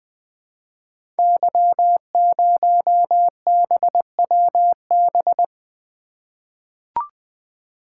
Morse code, heard as a keyed tone, about 20 wpm, 700 Hz. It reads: Y0BWB
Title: Code Groups - 5 Letter:Numbers - 20wpm